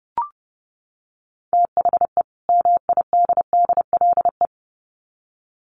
Morse code, heard as a keyed tone, about 30 wpm, 700 Hz. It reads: THE MIDDLE
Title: Sets of 2 Words 30wpm